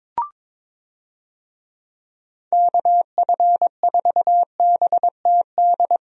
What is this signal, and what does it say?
Morse code, heard as a keyed tone, about 21 wpm, 700 Hz. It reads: KF4BTD
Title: USA Call Signs - 6 Characters - 22wpm